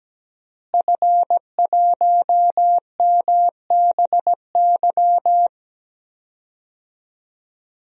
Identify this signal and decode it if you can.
Morse code, heard as a keyed tone, about 17 wpm, 700 Hz. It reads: F1MBY
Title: DXCC-100 Call Signs - 5 Characters - Repeated 3x - 17wpm